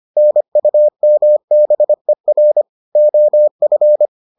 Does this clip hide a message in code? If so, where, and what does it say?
Yes: Morse code from 0.2 to 4.1 s, heard as a keyed tone, about 25 wpm, 600 Hz. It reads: NUMBER OF